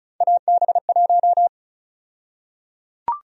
Morse code, heard as a keyed tone, about 34 wpm, 700 Hz. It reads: AB1
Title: USA Call Sign Prefixes - 35wpm